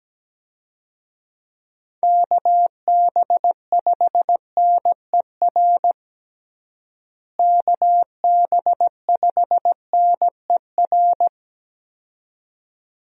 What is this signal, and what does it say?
Morse code, heard as a keyed tone, about 17 wpm, 700 Hz. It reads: KB5NER KB5NER
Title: USA Call Signs - 6 Characters - Repeated 2x - 17wpm